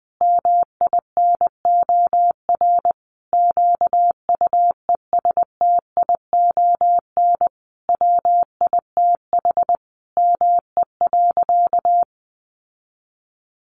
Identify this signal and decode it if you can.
Morse code, heard as a keyed tone, about 20 wpm, 700 Hz. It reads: MINOR QUESTION WITH ME.
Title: Sentences from Top 1500 Words 20wpm